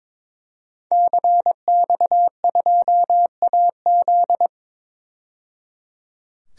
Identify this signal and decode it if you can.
Morse code, heard as a keyed tone, about 21 wpm, 700 Hz. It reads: CX2AZ
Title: DXCC-100 Call Signs - 5 Characters - 22wpm